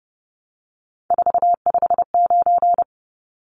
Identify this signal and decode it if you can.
Morse code, heard as a keyed tone, about 30 wpm, 700 Hz. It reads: 459
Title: Numbers - 3-Digit 30wpm